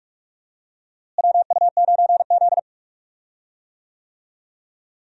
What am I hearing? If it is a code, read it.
Morse code, heard as a keyed tone, about 44 wpm, 700 Hz. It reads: WU9Z
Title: USA Call Signs - 4 Characters - Repeated 3x - 45wpm